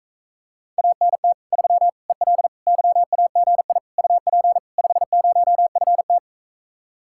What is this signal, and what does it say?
Morse code, heard as a keyed tone, about 44 wpm, 700 Hz. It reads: ANT 3 EL YAGI UP 50FT